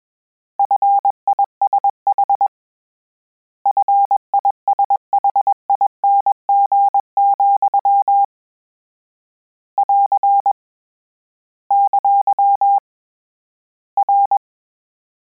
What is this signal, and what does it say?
Morse code, heard as a keyed tone, about 21 wpm, 800 Hz. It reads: FISH FISHING, <AR> ! R
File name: Combinations / Combinations-mp3s / Combinations@20